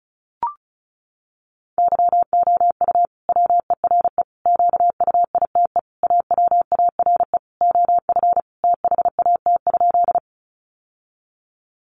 Morse code, heard as a keyed tone, about 34 wpm, 700 Hz. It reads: YOU WERE QUITE AWARE OF THAT?
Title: Sentences from Top 1200 Words 35wpm